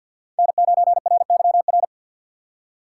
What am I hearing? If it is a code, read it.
Morse code, heard as a keyed tone, about 53 wpm, 700 Hz. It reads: N9RYR
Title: USA Call Signs - 5 Characters - 4x Character Spacing - 50wpm